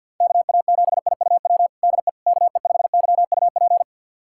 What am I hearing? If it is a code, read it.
Morse code, heard as a keyed tone, about 51 wpm, 700 Hz. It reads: KA7IUW DE KE5YUP
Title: USA Callsign DE Callsign - 6 Characters - 50wpm